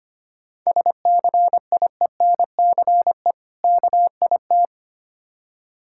Morse code, heard as a keyed tone, about 25 wpm, 700 Hz. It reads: SCIENCE KIT
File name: practice / 25wpm / Difficult Words with Hints - 25wpm